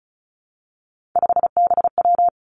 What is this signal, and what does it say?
Morse code, heard as a keyed tone, about 36 wpm, 700 Hz. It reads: 5BW
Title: Three Challenging Letter-Number Combinations 35wpm